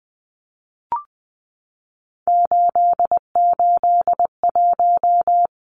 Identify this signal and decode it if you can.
Morse code, heard as a keyed tone, about 20 wpm, 700 Hz. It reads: 881
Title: Numbers - 3-Digit 20wpm